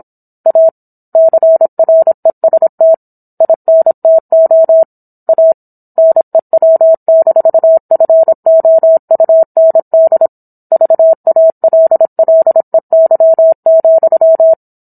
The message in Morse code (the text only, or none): A CREST INTO A NEW-FOUND VALLEY,